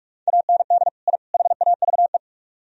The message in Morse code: AND I HAVE

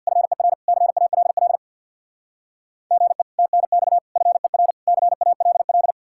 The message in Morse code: FER CALL GE TNX FER CALL

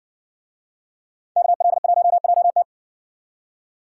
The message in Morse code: KL1JT

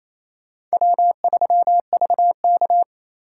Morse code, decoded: W3VK

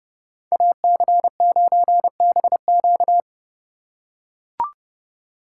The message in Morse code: AC9BQ

